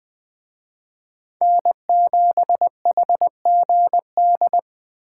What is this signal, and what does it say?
Morse code, heard as a keyed tone, about 21 wpm, 700 Hz. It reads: N7HGD
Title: Speed Racing - USA 1x3 Call Signs 17wpm